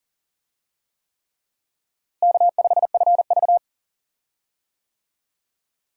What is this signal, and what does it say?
Morse code, heard as a keyed tone, about 40 wpm, 700 Hz. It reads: K5FV
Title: USA Call Signs - 4 Characters - Repeated 3x - 40wpm